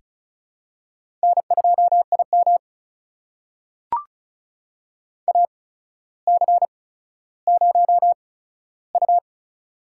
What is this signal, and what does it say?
Morse code, heard as a keyed tone, about 36 wpm, spaced as for 15 wpm, 700 Hz. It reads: N2IM AC0U
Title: USA Call Signs - 8x Character Spacing - 35wpm